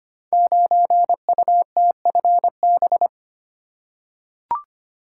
Morse code, heard as a keyed tone, about 25 wpm, 700 Hz. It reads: 9UTFB
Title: Code Groups - 5 Letter:Numbers - 25wpm